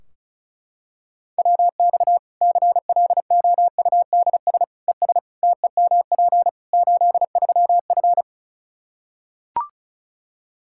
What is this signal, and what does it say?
Morse code, heard as a keyed tone, about 36 wpm, 700 Hz. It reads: WX CLOUDS ES TEMP 83F
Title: K1USN SST QSO Practice 35wpm